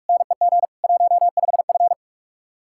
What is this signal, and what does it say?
Morse code, heard as a keyed tone, about 59 wpm, 700 Hz. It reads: NEG 15F